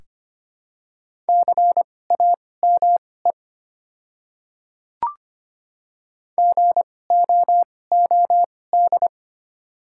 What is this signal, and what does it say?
Morse code, heard as a keyed tone, about 25 wpm, spaced as for 18 wpm, 700 Hz. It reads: CAME GOOD
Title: 2x Character Spacing - Top 200 Words - 25wpm